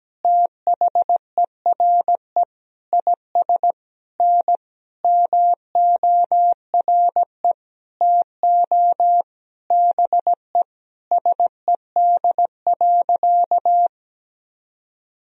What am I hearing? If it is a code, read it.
Morse code, heard as a keyed tone, about 17 wpm, 700 Hz. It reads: THERE IS N MORE TO BE SED.
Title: Sentences from Top 100 Words and Abbreviations 17wpm